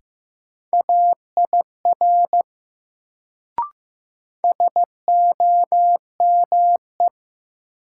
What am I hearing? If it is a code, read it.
Morse code, heard as a keyed tone, about 15 wpm, 700 Hz. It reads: AIR SOME
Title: Sets of 2 Words - New Word Drill - 15wpm